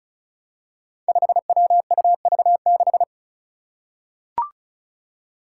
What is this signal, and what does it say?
Morse code, heard as a keyed tone, about 36 wpm, 700 Hz. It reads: 5WUV6
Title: Five Challenging Letter-Number Combinations 35wpm